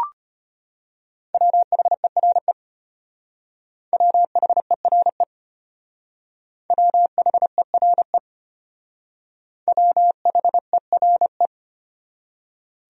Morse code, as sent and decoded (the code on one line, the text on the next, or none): .-- .... .-. / .-- .... .-. / .-- .... .-. / .-- .... .-. .
WHERE WHERE WHERE WHERE